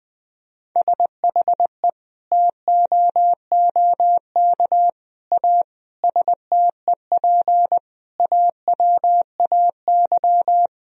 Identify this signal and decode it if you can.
Morse code, heard as a keyed tone, about 21 wpm, 700 Hz. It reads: SHE TOOK A STEP AWAY
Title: Sentences from Top 1000 Words - Review Part 2 - 20wpm